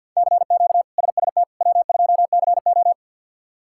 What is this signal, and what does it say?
Morse code, heard as a keyed tone, about 47 wpm, 690 Hz. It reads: CQ SST W2CO
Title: K1USN SST QSO Practice - 50wpm